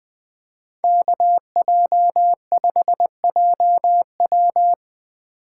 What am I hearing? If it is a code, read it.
Morse code, heard as a keyed tone, about 20 wpm, 700 Hz. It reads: KJ5JW
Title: Speed Racing - USA 2x2 Call Signs 17wpm